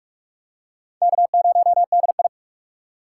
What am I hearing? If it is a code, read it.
Morse code, heard as a keyed tone, about 44 wpm, 700 Hz. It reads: K0DI
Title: USA Call Signs - 3x Character Spacing - 45wpm